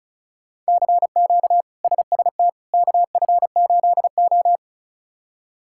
Morse code, heard as a keyed tone, about 36 wpm, 700 Hz. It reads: CQ SST KF8O
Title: K1USN SST QSO Practice - 35wpm